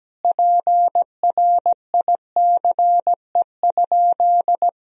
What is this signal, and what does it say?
Morse code, heard as a keyed tone, about 17 wpm, 700 Hz. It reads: PRICE?
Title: Sentences from Top 1000 Words - Review Part 4 - 17wpm